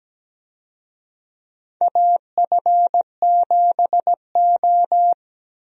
Morse code, heard as a keyed tone, about 17 wpm, 700 Hz. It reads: AF7O
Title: USA Call Signs - 4 Characters - Repeated 2x - 17wpm